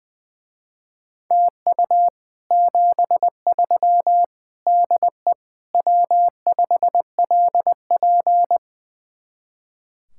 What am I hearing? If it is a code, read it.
Morse code, heard as a keyed tone, about 20 wpm, 700 Hz. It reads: TU 73 DE W5LP
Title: Speed-Racing Common QSO Phrases Encore - 20wpm